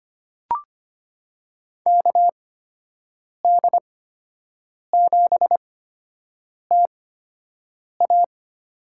Morse code, heard as a keyed tone, about 25 wpm, spaced as for 7 wpm, 700 Hz. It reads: KD7TA
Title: USA Call Signs - 6 Characters - 8x Character Spacing - 25wpm